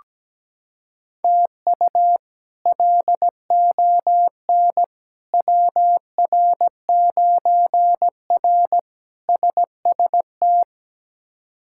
Morse code, heard as a keyed tone, about 17 wpm, 700 Hz. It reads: TU LON WR9R SST
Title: K1USN SST QSO Practice - 17wpm